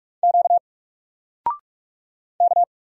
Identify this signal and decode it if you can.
Morse code, heard as a keyed tone, about 44 wpm, 700 Hz. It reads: Q K
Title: Single Letter-Number 45wpm